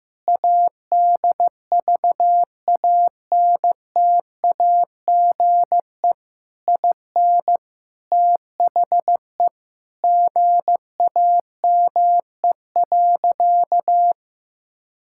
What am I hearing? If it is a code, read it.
Morse code, heard as a keyed tone, about 14 wpm, 700 Hz. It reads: ADVANTAGE IN THE GAME.
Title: Sentences from Top 1000 Words - Review Part 6 - 15wpm